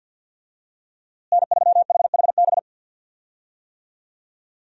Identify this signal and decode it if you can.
Morse code, heard as a keyed tone, about 52 wpm, 700 Hz. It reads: N3HHB
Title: USA Call Signs - 5 Characters - Repeated 2x - 50wpm